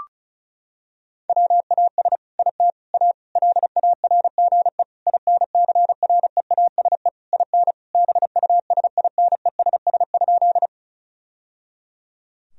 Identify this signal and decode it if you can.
Morse code, heard as a keyed tone, about 34 wpm, 700 Hz. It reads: WAS IT A LARGE INCREASE IN BUSINESS?
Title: Sentences from Top 1000 Words - Review Part 4 - 35wpm